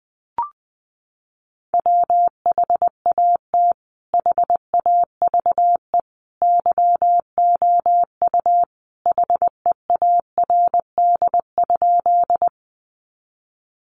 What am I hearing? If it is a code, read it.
Morse code, heard as a keyed tone, about 19 wpm, 700 Hz. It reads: WHAT HAVE YOU HEARD?
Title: Sentences from Top 400 Words 20wpm